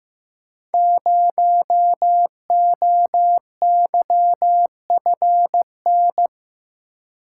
Morse code, heard as a keyed tone, about 15 wpm, 700 Hz. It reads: 0OYFN
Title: Code Groups - 5 Letter:Numbers - 15wpm